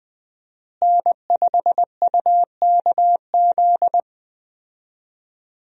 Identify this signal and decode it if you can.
Morse code, heard as a keyed tone, about 21 wpm, 700 Hz. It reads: N5UKZ
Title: USA Call Signs - 5 Characters - Repeated 2x - 20wpm